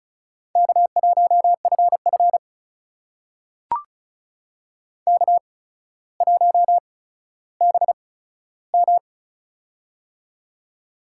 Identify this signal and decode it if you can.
Morse code, heard as a keyed tone, about 36 wpm, spaced as for 15 wpm, 700 Hz. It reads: K1FF K1BM
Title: USA Call Signs - 4 Characters - 8x Character Spacing - 35wpm